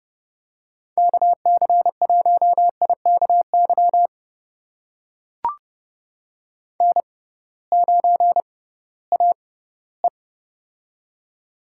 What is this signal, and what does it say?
Morse code, heard as a keyed tone, about 32 wpm, spaced as for 18 wpm, 700 Hz. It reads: KC1IKY N9AE
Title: USA Call Signs - 6x Character Spacing - 30wpm